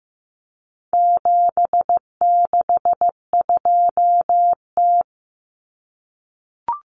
Morse code, heard as a keyed tone, about 15 wpm, 700 Hz. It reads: 762T
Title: Common Cut Numbers - 4-Digit 15wpm